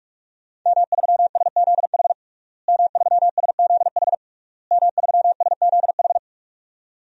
Morse code, heard as a keyed tone, about 41 wpm, 700 Hz. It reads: M3SZH M3SZH M3SZH
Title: DXCC-100 Call Signs - 5 Characters - Repeated 3x - 45wpm